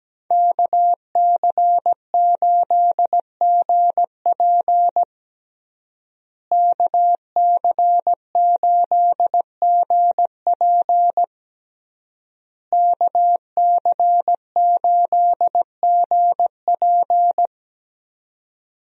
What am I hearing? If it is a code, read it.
Morse code, heard as a keyed tone, about 17 wpm, 700 Hz. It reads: KC8GP KC8GP KC8GP
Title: USA Call Signs - 5 Characters - Repeated 3x - 17wpm